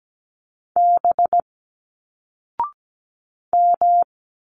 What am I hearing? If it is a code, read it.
Morse code, heard as a keyed tone, about 17 wpm, 700 Hz. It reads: B M
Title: Single Letters 17wpm